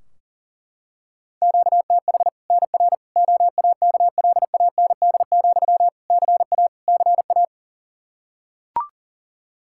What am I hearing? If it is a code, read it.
Morse code, heard as a keyed tone, about 37 wpm, 700 Hz. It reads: QTH NR OAKLAND, CA CA
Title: K1USN SST QSO Practice 40wpm